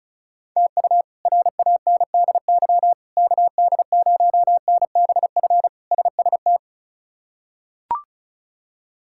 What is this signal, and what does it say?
Morse code, heard as a keyed tone, about 34 wpm, 700 Hz. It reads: TU RANDY KD0NBF SST
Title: K1USN SST QSO Practice - 35wpm